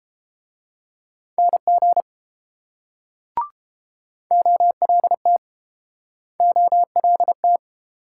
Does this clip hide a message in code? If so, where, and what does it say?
Yes: Morse code from 1.4 to 7.6 s, heard as a keyed tone, about 32 wpm, 700 Hz. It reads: NG OLT OLT